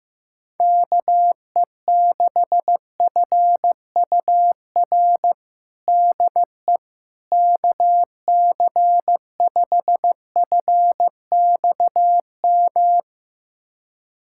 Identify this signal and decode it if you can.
Morse code, heard as a keyed tone, about 15 wpm, 700 Hz. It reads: KE6FUR DE KC5FXM
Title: USA Callsign DE Callsign 15wpm